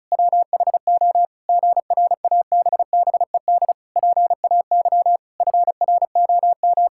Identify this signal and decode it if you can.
Morse code, heard as a keyed tone, about 34 wpm, 700 Hz. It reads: WHO GRABBED PAY FROM
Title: Pangram Challenge v2 - 35wpm